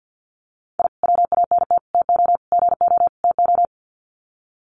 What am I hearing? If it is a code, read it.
Morse code, heard as a keyed tone, about 50 wpm, 700 Hz. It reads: I WANT TO GO TO
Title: Sets of 5 Words Encore 50wpm